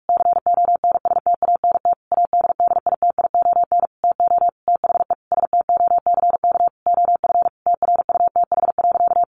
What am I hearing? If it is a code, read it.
Morse code, heard as a keyed tone, about 44 wpm, 700 Hz. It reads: CONSTANT ADDITION TO THE STOCK OF TRUTH.
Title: Sentences from Top 1500 Words 45wpm